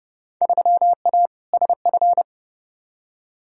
Morse code, heard as a keyed tone, about 30 wpm, 700 Hz. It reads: 3A SF
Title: Field Day Exchanges Encore - 30wpm